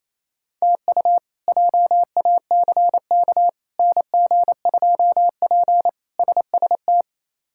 Morse code, heard as a keyed tone, about 28 wpm, 700 Hz. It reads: TU JACK NG2P SST
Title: K1USN SST QSO Practice - 28wpm